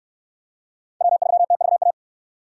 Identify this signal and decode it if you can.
Morse code, heard as a keyed tone, about 67 wpm, 700 Hz. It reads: W3TVA